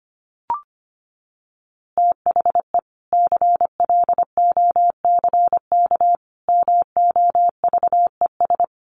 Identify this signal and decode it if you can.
Morse code, heard as a keyed tone, about 25 wpm, 700 Hz. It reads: THE CLOCK MOVES
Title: Easy Phrases 25wpm